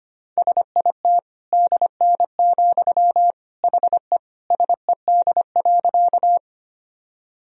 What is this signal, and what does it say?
Morse code, heard as a keyed tone, about 25 wpm, 700 Hz. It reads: SIT DN, HE SED.
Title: Sentences from Top 500 Words and Abbreviations 25wpm